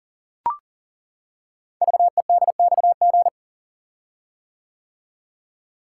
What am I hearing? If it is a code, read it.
Morse code, heard as a keyed tone, about 42 wpm, 690 Hz. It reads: VEDXG
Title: Code Groups - 5 Letters - 40wpm